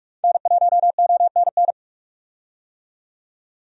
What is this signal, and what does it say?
Morse code, heard as a keyed tone, about 46 wpm, 700 Hz. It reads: N1ONN
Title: USA Call Signs - 5 Characters - 45wpm